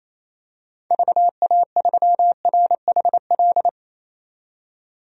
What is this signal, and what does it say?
Morse code, heard as a keyed tone, about 28 wpm, 700 Hz. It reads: VA3RHL
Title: DXCC-100 Call Signs - 6 Characters - 28wpm